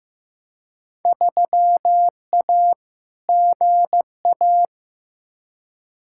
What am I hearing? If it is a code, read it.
Morse code, heard as a keyed tone, about 15 wpm, 700 Hz. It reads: 3A GA
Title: Field Day Exchanges - 15wpm